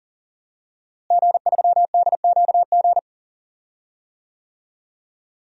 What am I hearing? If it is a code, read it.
Morse code, heard as a keyed tone, about 40 wpm, 700 Hz. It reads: G3DQG
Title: DXCC-100 Call Signs - 5 Characters - Repeated 3x - 40wpm